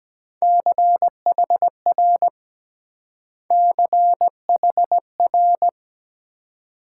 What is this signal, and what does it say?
Morse code, heard as a keyed tone, about 21 wpm, 700 Hz. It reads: CHR CHR